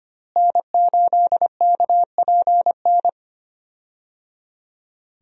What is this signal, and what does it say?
Morse code, heard as a keyed tone, about 25 wpm, 700 Hz. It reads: N8KPN
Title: USA Call Signs - 5 Characters - Repeated 3x - 25wpm